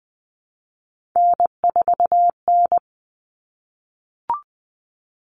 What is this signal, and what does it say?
Morse code, heard as a keyed tone, about 20 wpm, 700 Hz. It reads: N4N
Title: Common Cut Numbers - 3-Digit 20wpm